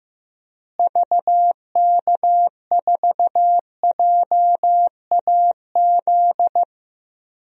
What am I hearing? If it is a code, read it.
Morse code, heard as a keyed tone, about 15 wpm, 700 Hz. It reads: VK4JAZ
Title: DXCC-100 Call Signs - 6 Characters - Repeated 3x - 15wpm